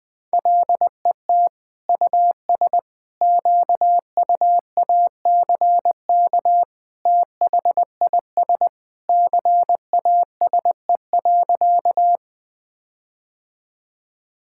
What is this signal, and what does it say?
Morse code, heard as a keyed tone, about 20 wpm, 700 Hz. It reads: LET US QUACK THIS CASE.
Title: All-Age Appropriate Jokes - 20wpm